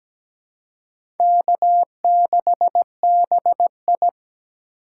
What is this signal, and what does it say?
Morse code, heard as a keyed tone, about 17 wpm, 700 Hz. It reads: K6BI
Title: USA Call Signs - 4 Characters - 2x Character Spacing - 17wpm